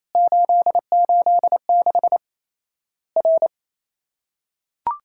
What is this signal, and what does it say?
Morse code, heard as a keyed tone, about 28 wpm, 700 Hz. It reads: 886 R
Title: Worked All Europe DX Contest - QTC Exchange - 28wpm